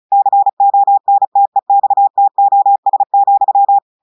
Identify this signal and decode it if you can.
Morse code, heard as a keyed tone, about 36 wpm, 800 Hz. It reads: CONTEXTOS,
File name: audios / portugais / portugais texte en clair 1 vitesse 35 mots minutes